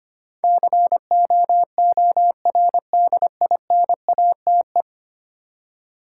Morse code, heard as a keyed tone, about 25 wpm, 700 Hz. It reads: COORDINATE
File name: Ten Letter Words - 25wpm